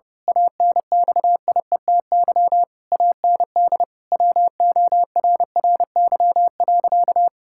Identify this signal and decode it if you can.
Morse code, heard as a keyed tone, about 30 wpm, 700 Hz. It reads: ANXIETY AND WORRY.
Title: Pangram Challenge v2 - 30wpm